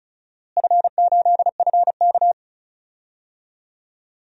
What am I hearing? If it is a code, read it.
Morse code, heard as a keyed tone, about 36 wpm, 700 Hz. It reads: F8FK
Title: DXCC-100 Call Signs - 4 Characters - Repeated 2x - 35wpm